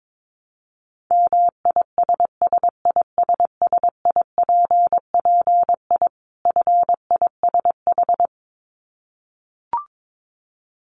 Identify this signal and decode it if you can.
Morse code, heard as a keyed tone, about 22 wpm, 700 Hz. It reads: MISSISSIPPI FISH
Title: Dit Challenge 22wpm